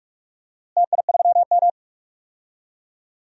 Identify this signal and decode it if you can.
Morse code, heard as a keyed tone, about 42 wpm, 700 Hz. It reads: TI3M
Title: DXCC-100 Call Signs - 4 Characters - Repeated 2x - 45wpm